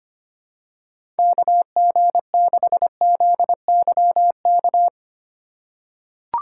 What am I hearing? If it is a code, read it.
Morse code, heard as a keyed tone, about 25 wpm, 700 Hz. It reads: KG6ZYK
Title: USA Call Signs - 3x Character Spacing - 25wpm